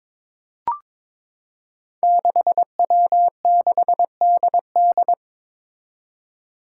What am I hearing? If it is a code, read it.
Morse code, heard as a keyed tone, about 22 wpm, 700 Hz. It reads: 6W6DD
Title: Five Challenging Letter-Number Combinations 22wpm